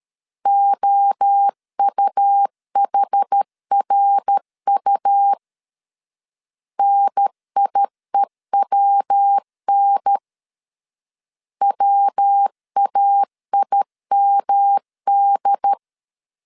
Morse code, heard as a keyed tone, about 13 wpm, 800 Hz. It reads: OUHRU NIEWN WAIMD
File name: CW / DET / 1 3 1500C S B 12wpm 2828